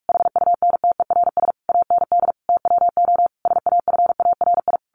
Morse code, heard as a keyed tone, about 46 wpm, 700 Hz. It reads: HUNTERS AND TWO SAFARI
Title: A Sound Of Thunder 45wpm